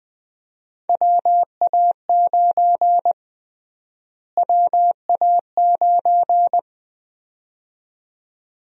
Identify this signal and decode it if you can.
Morse code, heard as a keyed tone, about 20 wpm, 700 Hz. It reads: WA9 WA9
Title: USA Call Sign Prefixes - Repeated 2x - 20wpm